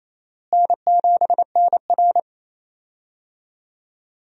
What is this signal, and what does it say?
Morse code, heard as a keyed tone, about 27 wpm, 700 Hz. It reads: N7NR
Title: USA Call Signs - 4 Characters - Repeated 2x - 28wpm